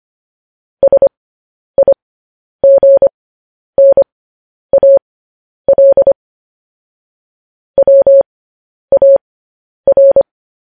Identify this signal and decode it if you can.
Morse code, heard as a keyed tone, about 25 wpm, spaced as for 10 wpm, 550 Hz. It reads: SIGNAL WAR
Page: news-headlines-in-morse-code-at-10-wpm